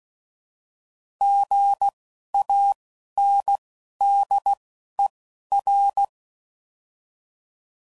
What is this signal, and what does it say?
Morse code, heard as a keyed tone, about 16 wpm, spaced as for 11 wpm, 800 Hz. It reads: GANDER
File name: Prefixes / By First Letter